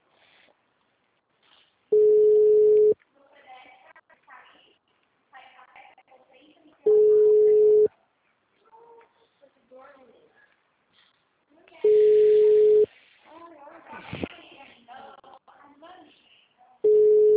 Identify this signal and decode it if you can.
Ringback tone (Europe, 425 Hz), with speech.
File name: out- -2030- - - .21606